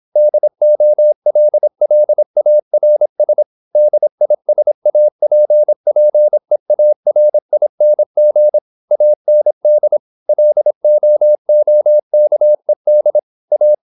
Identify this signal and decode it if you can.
Morse code, heard as a keyed tone, about 26 wpm, 600 Hz. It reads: DOLLARS DISAPPEARING AND LOOKED A
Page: Never 26wpm | CW med Gnister